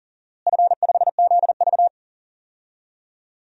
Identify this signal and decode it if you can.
Morse code, heard as a keyed tone, about 40 wpm, 700 Hz. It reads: F5ZV
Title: DXCC-100 Call Signs - 4 Characters - Repeated 2x - 40wpm